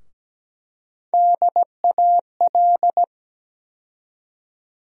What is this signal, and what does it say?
Morse code, heard as a keyed tone, about 17 wpm, 700 Hz. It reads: DAL
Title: USA Call Sign Postfixes - 3 Letters 17wpm